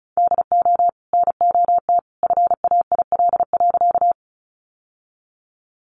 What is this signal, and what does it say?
Morse code, heard as a keyed tone, about 34 wpm, 700 Hz. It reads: DO NOT FAIL.
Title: Sentences from Top 1100 Words 35wpm